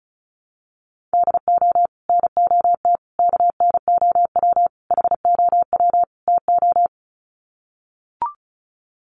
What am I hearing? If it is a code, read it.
Morse code, heard as a keyed tone, about 34 wpm, 700 Hz. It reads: DO NOT KNOW HOW TO
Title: Sets of 5 Words 35wpm